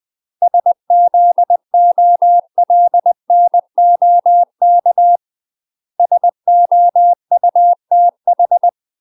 Morse code, heard as a keyed tone, about 20 wpm, 700 Hz. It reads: SZOLNOK SOUTH
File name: UK MASTER QSOs TRAINING / 20WPM / UK MASTER QSOs 0069